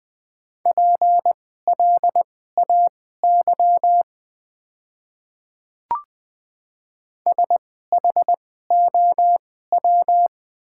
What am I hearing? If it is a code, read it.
Morse code, heard as a keyed tone, about 20 wpm, spaced as for 14 wpm, 700 Hz. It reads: PLAY SHOW
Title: 2x Character Spacing - Top 200 Words - 20wpm